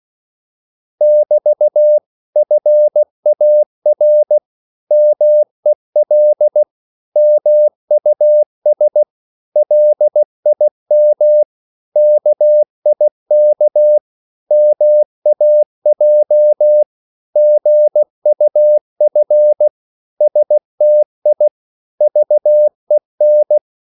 Morse code, heard as a keyed tone, about 16 wpm, 600 Hz. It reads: <BT> FAR MEL MUS LIM KIK MAJ GUF STI VEN
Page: Korte danske ord 16wpm